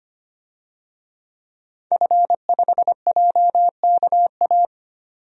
Morse code, heard as a keyed tone, about 25 wpm, 700 Hz. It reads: F5JKA
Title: DXCC-100 Call Signs - 5 Characters - 25wpm